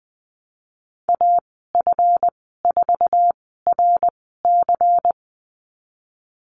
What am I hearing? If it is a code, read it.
Morse code, heard as a keyed tone, about 19 wpm, spaced as for 14 wpm, 700 Hz. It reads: AF4RC
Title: 147 Lesson 24g - Letter y - Call Signs 20wpm